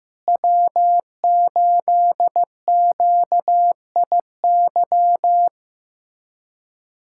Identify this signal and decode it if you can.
Morse code, heard as a keyed tone, about 15 wpm, 700 Hz. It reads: W8QIY
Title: USA Callsign DE Callsign - 5 Characters - 15wpm